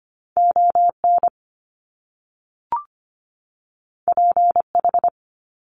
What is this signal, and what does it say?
Morse code, heard as a keyed tone, about 25 wpm, 700 Hz. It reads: ON PH
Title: Top Two Letter Combinations in Words 25wpm